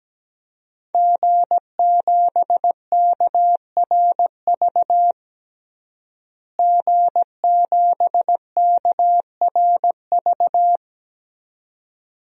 Morse code, heard as a keyed tone, about 17 wpm, 700 Hz. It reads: G7KRV G7KRV
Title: DXCC-100 Call Signs - 5 Characters - Repeated 2x - 17wpm